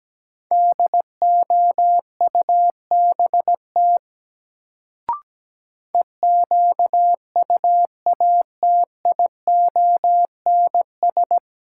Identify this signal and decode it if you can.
Morse code, heard as a keyed tone, about 17 wpm, 700 Hz. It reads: DOUBT EQUATIONS
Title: Sets of 5 Words - New Word Drill - 17wpm